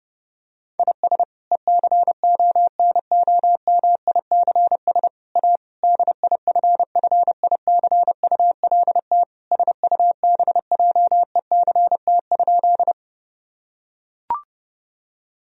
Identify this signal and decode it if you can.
Morse code, heard as a keyed tone, about 32 wpm, 700 Hz. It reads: IS ECONOMICS A DIFFICULT SUBJECT?
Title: Sentences from Top 1000 Words - Review Part 3 - 30wpm